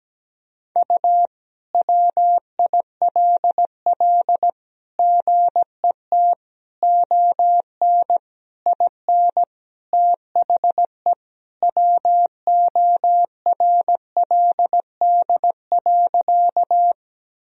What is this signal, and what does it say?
Morse code, heard as a keyed tone, about 17 wpm, 700 Hz. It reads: U WILL GET ON IN THE WORLD.
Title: Sentences from Top 200 Words and Abbreviations 17wpm